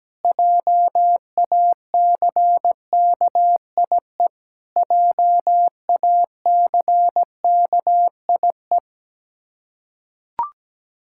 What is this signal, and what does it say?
Morse code, heard as a keyed tone, about 17 wpm, 700 Hz. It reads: JACKIE JACKIE
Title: Practice QSO - Exchange 1 - Beginning - 17wpm